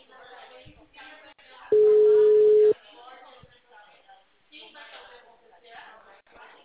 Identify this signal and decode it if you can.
Ringback tone (Europe, 425 Hz), with speech.
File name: out- -2004- - - .97236